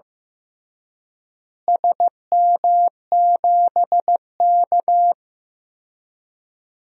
Morse code, heard as a keyed tone, about 15 wpm, 700 Hz. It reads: SM7K
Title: DXCC-100 Call Signs - 4 Characters - Repeated 2x - 15wpm